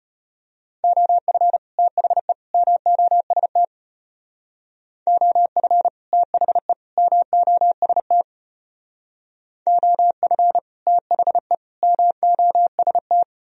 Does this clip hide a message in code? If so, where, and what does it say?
Yes: Morse code from 0.8 to 13.2 s, heard as a keyed tone, about 32 wpm, 700 Hz. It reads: OF THE MOST OF THE MOST OF THE MOST